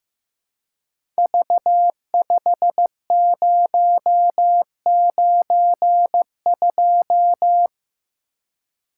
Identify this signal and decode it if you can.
Morse code, heard as a keyed tone, about 15 wpm, 700 Hz. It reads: V5092
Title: Code Groups - 5 Letter:Numbers - 15wpm